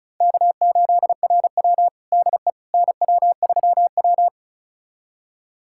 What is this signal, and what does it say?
Morse code, heard as a keyed tone, about 34 wpm, 700 Hz. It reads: K8RW DE NW3W
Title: USA Callsign DE Callsign - 4 Characters - 35wpm